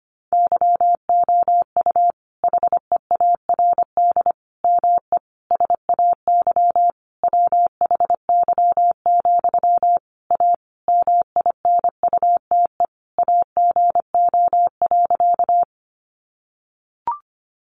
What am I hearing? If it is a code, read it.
Morse code, heard as a keyed tone, about 25 wpm, 700 Hz. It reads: YOU HEARD ME SAY WHY, A MINUTE AGO.
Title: Sentences from Top 500 Words 25wpm